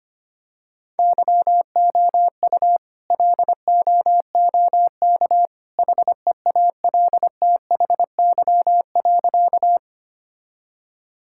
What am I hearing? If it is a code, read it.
Morse code, heard as a keyed tone, about 25 wpm, 700 Hz. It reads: YOU LOOK HEALTHY.
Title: Sentences from Top 1000 Words - Review Part 5 - 25wpm